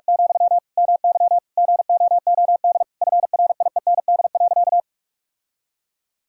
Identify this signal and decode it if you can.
Morse code, heard as a keyed tone, about 46 wpm, 700 Hz. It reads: , MY GOOD FRIEND.
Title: Sentences from Top 1600 Words 45wpm